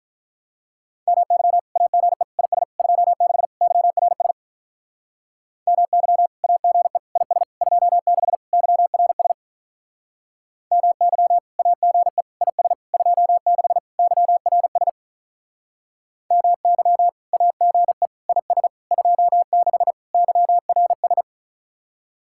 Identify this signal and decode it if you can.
Morse code, heard as a keyed tone, about 45 wpm, 700 Hz. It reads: MY AGE IS 26 YRS MY AGE IS 26 YRS MY AGE IS 26 YRS MY AGE IS 26 YRS